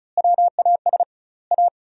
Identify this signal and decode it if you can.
Morse code, heard as a keyed tone, about 34 wpm, 700 Hz. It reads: WAS A
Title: 2x Word-Spacing - Sentences from Top 500 Words - 35wpm